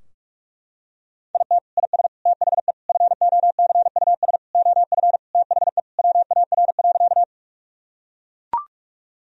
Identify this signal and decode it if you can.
Morse code, heard as a keyed tone, about 44 wpm, 700 Hz. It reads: IT IS THE FOCUS OF THE WAR.
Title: Sentences from Top 1000 Words - Review Part 4 - 45wpm